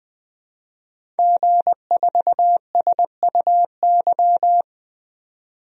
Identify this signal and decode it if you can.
Morse code, heard as a keyed tone, about 21 wpm, 700 Hz. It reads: G4SUY
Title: DXCC-100 Call Signs - 5 Characters - 20wpm